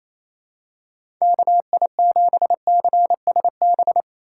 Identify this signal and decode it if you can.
Morse code, heard as a keyed tone, about 28 wpm, 700 Hz. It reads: KI7CSB
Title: USA Call Signs - 6 Characters - 2x Character Spacing - 28wpm